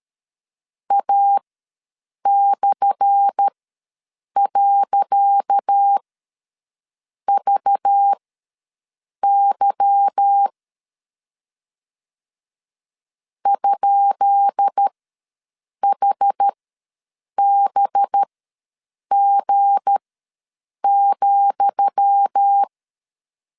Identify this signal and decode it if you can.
Morse code, heard as a keyed tone, about 13 wpm, spaced as for 7 wpm, 800 Hz. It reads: A/.VY ?HBG,
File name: LEZIONI / 10 SIGNS / 10D COMA ws 12wpm 3618